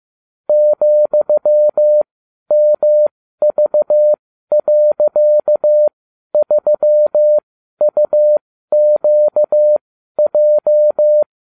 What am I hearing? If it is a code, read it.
Morse code, heard as a keyed tone, about 15 wpm, spaced as for 12 wpm, 600 Hz. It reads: ,MV.3UQJ